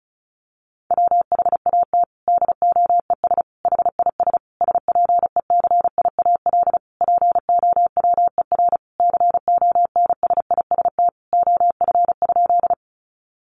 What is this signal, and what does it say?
Morse code, heard as a keyed tone, about 34 wpm, 700 Hz. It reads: WHAT DOES HIS SPECIAL POWER CONSIST OF?
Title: Sentences from Top 1400 Words 35wpm